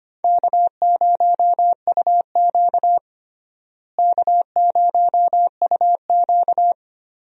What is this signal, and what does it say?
Morse code, heard as a keyed tone, about 25 wpm, 700 Hz. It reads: K0UQ K0UQ
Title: USA Call Signs - 4 Characters - Repeated 3x - 25wpm